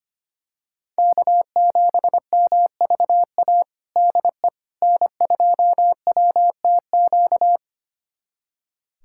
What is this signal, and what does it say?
Morse code, heard as a keyed tone, about 25 wpm, 700 Hz. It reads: K7MVA DE N2WTQ
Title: USA Callsign DE Callsign - 5 Characters - 25wpm